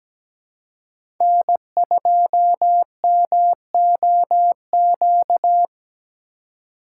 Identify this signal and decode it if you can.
Morse code, heard as a keyed tone, about 17 wpm, 700 Hz. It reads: N2MOQ
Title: USA Call Signs - 5 Characters - 3x Character Spacing - 17wpm